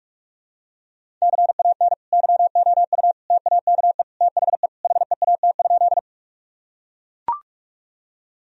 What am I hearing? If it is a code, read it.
Morse code, heard as a keyed tone, about 41 wpm, 700 Hz. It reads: CAN YOU TAKE THE HEAT?
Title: Sentences from Top 1000 Words - Review Part 3 - 45wpm